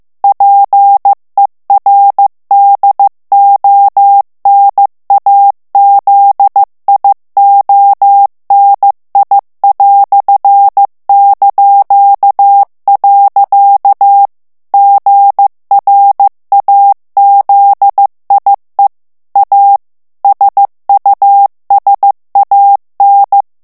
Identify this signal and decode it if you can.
Morse code, heard as a keyed tone, about 15 wpm, 800 Hz. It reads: PER DONAZIONI"). GRAZIE A SUSAN
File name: ITALIEN texte clair 1 vitesse 14 mots minute